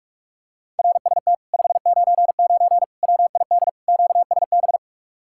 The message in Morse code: RST 599 WID QSB

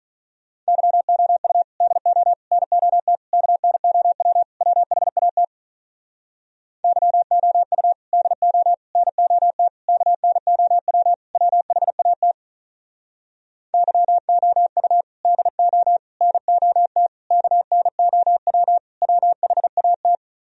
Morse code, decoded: YOU DO NOT KNOW WHAT YOU DO NOT KNOW WHAT YOU DO NOT KNOW WHAT

